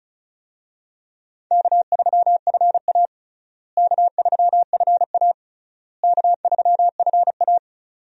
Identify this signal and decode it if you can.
Morse code, heard as a keyed tone, about 36 wpm, 700 Hz. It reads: K3FA K3FA K3FA
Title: USA Call Signs - 4 Characters - Repeated 3x - 35wpm